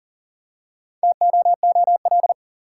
Morse code, heard as a keyed tone, about 40 wpm, 700 Hz. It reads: TOOL
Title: Four Letter Words - 40wpm